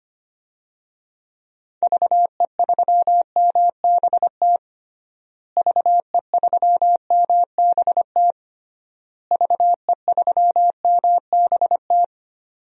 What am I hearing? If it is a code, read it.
Morse code, heard as a keyed tone, about 25 wpm, 700 Hz. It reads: VE3MBT VE3MBT VE3MBT
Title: DXCC-100 Call Signs - 6 Characters - Repeated 3x - 25wpm